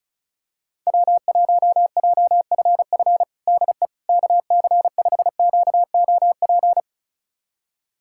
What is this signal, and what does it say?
Morse code, heard as a keyed tone, about 34 wpm, 700 Hz. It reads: W1JFF DE KC5QOP
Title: USA Callsign DE Callsign 35wpm